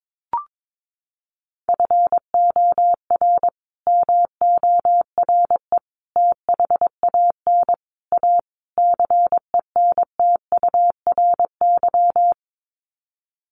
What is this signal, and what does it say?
Morse code, heard as a keyed tone, about 22 wpm, 700 Hz. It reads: FOR MORE THAN A CENTURY
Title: Sets of 5 Words Encore2 22wpm